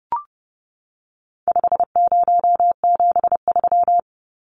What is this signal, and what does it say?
Morse code, heard as a keyed tone, about 30 wpm, 700 Hz. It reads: 5073
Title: Numbers - 4-Digit 30wpm